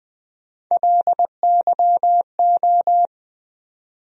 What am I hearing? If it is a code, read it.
Morse code, heard as a keyed tone, about 20 wpm, 700 Hz. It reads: LYO
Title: USA Call Sign Postfixes - 3 Letters 20wpm